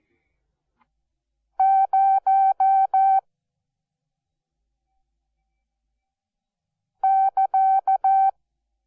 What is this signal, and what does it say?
Morse code, heard as a keyed tone, about 14 wpm, 770 Hz. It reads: 0 <KA>
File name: Onlinelehrgang / morsen / DJ4UF Morsen12